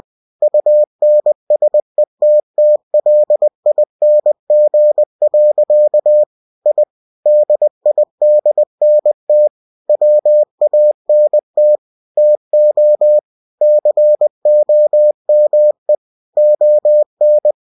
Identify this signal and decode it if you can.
Morse code, heard as a keyed tone, about 20 wpm, 600 Hz. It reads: UNSETTLING. I DIDNT WANT TO COME ON
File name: cw-books / winnie-the-pooh / winnie-20wpm-page-0079